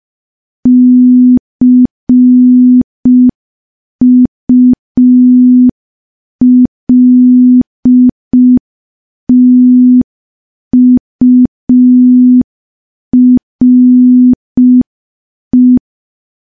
Morse code, heard as a keyed tone, about 5 wpm, 250 Hz. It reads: CULTURE
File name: morse / morse-training-05WPM